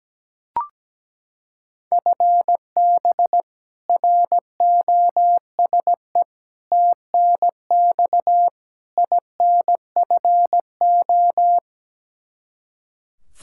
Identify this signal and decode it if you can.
Morse code, heard as a keyed tone, about 17 wpm, 700 Hz. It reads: FB ROSE TNX INFO
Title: Practice QSO - Exchange 3 - More Info - 17wpm